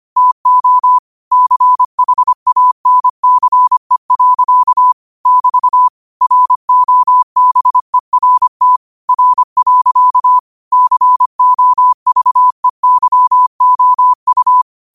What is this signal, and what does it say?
Morse code, heard as a keyed tone, about 25 wpm, 1000 Hz. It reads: TO CHANCE. <BT> ROBERT R. COVEYOU
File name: qotdfiles / 25wpm